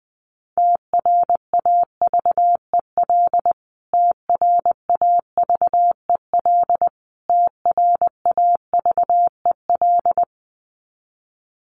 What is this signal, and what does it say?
Morse code, heard as a keyed tone, about 20 wpm, 700 Hz. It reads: TRAVEL TRAVEL TRAVEL
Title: Top 500 Words Repeated 20wpm